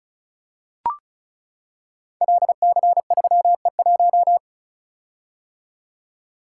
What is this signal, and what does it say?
Morse code, heard as a keyed tone, about 34 wpm, 700 Hz. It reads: LC3E1
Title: Code Groups - 5 Letter:Numbers - 35wpm